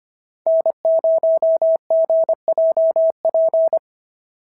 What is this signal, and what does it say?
Morse code, heard as a keyed tone, about 25 wpm, 650 Hz. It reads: N0GJP
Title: SOTA QSO Practice 25wpm